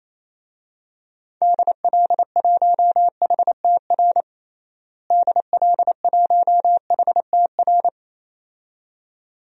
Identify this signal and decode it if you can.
Morse code, heard as a keyed tone, about 28 wpm, 700 Hz. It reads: DL1HTR DL1HTR
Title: DXCC-100 Call Signs - 6 Characters - Repeated 2x - 28wpm